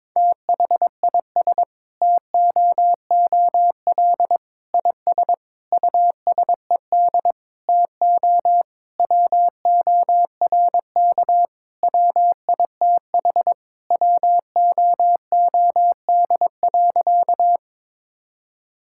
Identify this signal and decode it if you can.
Morse code, heard as a keyed tone, about 22 wpm, 700 Hz. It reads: THIS TOOL IS USED TO WORK WITH WOOD.
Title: Sentences from Top 1000 Words - Review Part 6 - 22wpm